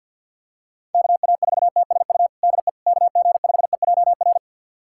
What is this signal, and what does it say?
Morse code, heard as a keyed tone, about 49 wpm, 700 Hz. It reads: KA4TSU DE KG5EJR
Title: USA Callsign DE Callsign - 6 Characters - 50wpm